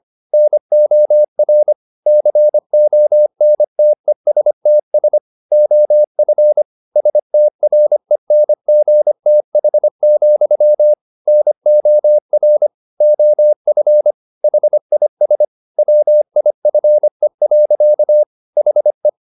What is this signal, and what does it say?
Morse code, heard as a keyed tone, about 25 wpm, 600 Hz. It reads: NOR CONTESTS OF STRENGTH, NOR OF HIS WIFE. HE
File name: the old man and the sea part 1 25wpm